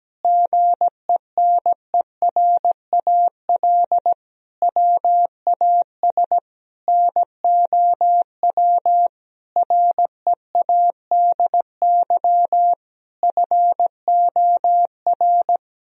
Morse code, heard as a keyed tone, about 17 wpm, 700 Hz. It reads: GENERAL WAS NOW READY FOR
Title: Sentences from Top 1000 Words - Review Part 4 - 17wpm